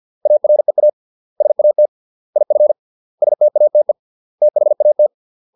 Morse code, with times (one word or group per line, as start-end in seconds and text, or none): 0.3-0.9 AREA
1.4-1.9 SAT
2.4-2.7 IF
3.2-3.9 STATE
4.4-5.1 THAT